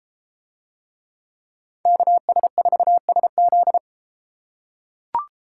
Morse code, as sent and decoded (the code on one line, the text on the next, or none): -.- ... ....- ... --..
KS4SZ